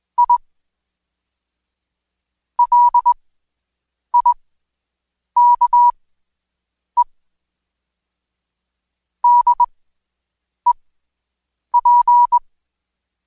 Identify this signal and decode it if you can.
Morse code, heard as a keyed tone, about 21 wpm, spaced as for 7 wpm, 970 Hz. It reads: I LIKE DEP
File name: Code Quick / AudioDownld2 / Seg113-Day18-DayDrill 18